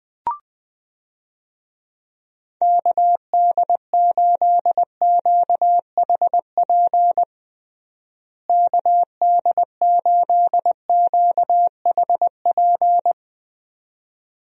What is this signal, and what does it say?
Morse code, heard as a keyed tone, about 20 wpm, 700 Hz. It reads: KD8QHP KD8QHP
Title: USA Call Signs - 6 Characters - Repeated 2x - 20wpm